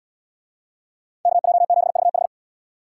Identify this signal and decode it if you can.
Morse code, heard as a keyed tone, about 77 wpm, 700 Hz. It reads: KJ7VH